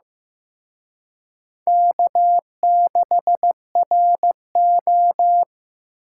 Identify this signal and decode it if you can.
Morse code, heard as a keyed tone, about 18 wpm, 700 Hz. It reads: K6RO
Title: Speed Racing - USA 1x2 Call Signs 15wpm